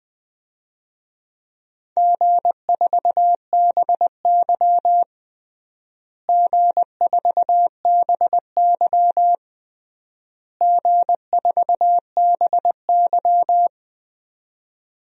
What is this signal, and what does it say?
Morse code, heard as a keyed tone, about 21 wpm, 700 Hz. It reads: G4BY G4BY G4BY
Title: DXCC-100 Call Signs - 4 Characters - Repeated 3x - 20wpm